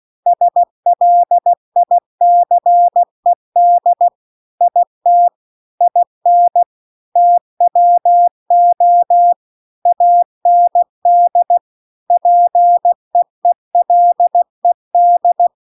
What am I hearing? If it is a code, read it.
Morse code, heard as a keyed tone, about 16 wpm, 700 Hz. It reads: SLICED IT IN TWO AND PEELED